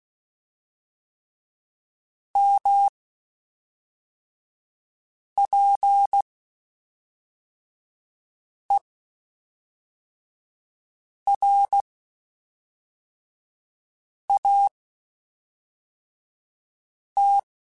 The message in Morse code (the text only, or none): M P E R A T